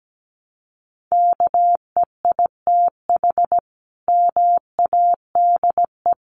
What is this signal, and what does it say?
Morse code, heard as a keyed tone, about 17 wpm, 700 Hz. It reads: KEITH MADE
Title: A Sound Of Thunder 17wpm